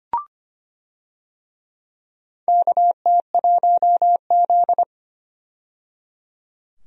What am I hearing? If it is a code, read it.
Morse code, heard as a keyed tone, about 25 wpm, 700 Hz. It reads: KT1Z
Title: USA Call Signs - 4 Characters - 25wpm